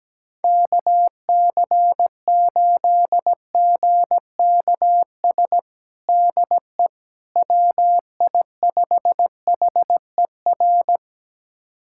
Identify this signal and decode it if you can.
Morse code, heard as a keyed tone, about 17 wpm, 700 Hz. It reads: KC8GKS DE WI5HER
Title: USA Callsign DE Callsign 17wpm